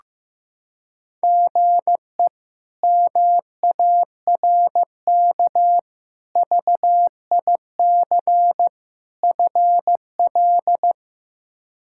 Morse code, heard as a keyed tone, about 15 wpm, 700 Hz. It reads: GE MARK VIC FL
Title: K1USN SST QSO Practice - 15wpm